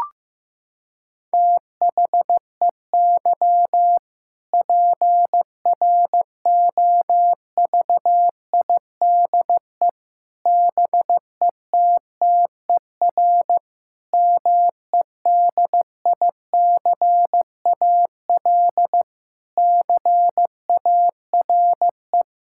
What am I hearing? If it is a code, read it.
Morse code, heard as a keyed tone, about 15 wpm, 700 Hz. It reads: THEY PROVIDE BETTER MEDICAL CARE
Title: Sentences from Top 1000 Words - Review Part 5 - 15wpm